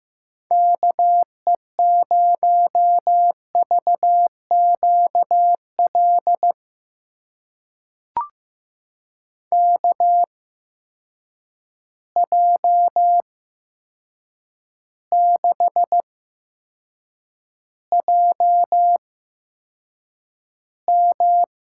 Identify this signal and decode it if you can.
Morse code, heard as a keyed tone, about 15 wpm, spaced as for 7 wpm, 700 Hz. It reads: KE0VQL KJ6JM
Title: USA Call Signs - 6 Characters - 8x Character Spacing - 15wpm